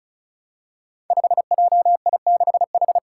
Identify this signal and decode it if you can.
Morse code, heard as a keyed tone, about 36 wpm, 700 Hz. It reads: 5JI6H
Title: Five Challenging Letter-Number Combinations 35wpm